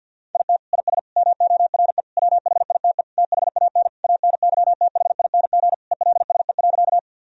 Morse code, heard as a keyed tone, about 52 wpm, 700 Hz. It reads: IT IS MORE WHITE THAN ANYTHING ELSE.
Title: Sentences from Top 1000 Words - Review Part 1 - 50wpm